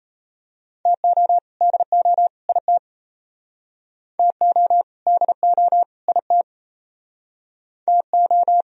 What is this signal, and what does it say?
Morse code, heard as a keyed tone, about 31 wpm, 700 Hz. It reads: TO DO IT TO DO IT TO
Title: Speed-Racing - Sets of 3 Words 28wpm